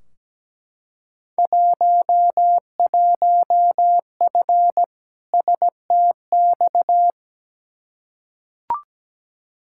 Morse code, heard as a keyed tone, about 17 wpm, 700 Hz. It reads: 11F STX
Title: Field Day Exchanges Encore - 17wpm